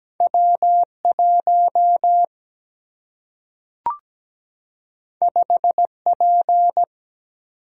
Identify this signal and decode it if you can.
Morse code, heard as a keyed tone, about 17 wpm, 700 Hz. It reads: W 1 5 P
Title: Single-Letter Single-Number 17wpm